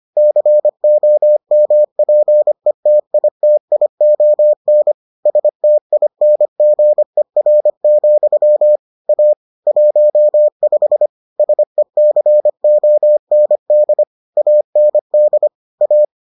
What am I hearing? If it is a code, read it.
Morse code, heard as a keyed tone, about 25 wpm, 600 Hz. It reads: COMPETITION STINGER, A 15 SECOND AND A